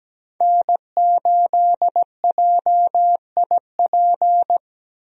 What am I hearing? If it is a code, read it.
Morse code, heard as a keyed tone, about 17 wpm, 700 Hz. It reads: N8JIP
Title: USA Call Signs - 5 Characters - Repeated 2x - 17wpm